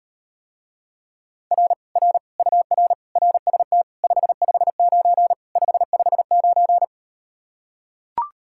Morse code, heard as a keyed tone, about 38 wpm, 700 Hz. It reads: R R UR RST 559 559